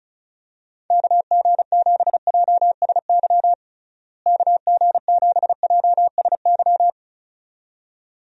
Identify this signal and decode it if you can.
Morse code, heard as a keyed tone, about 34 wpm, 700 Hz. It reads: KG7JSY KG7JSY
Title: USA Call Signs - 6 Characters - Repeated 2x - 35wpm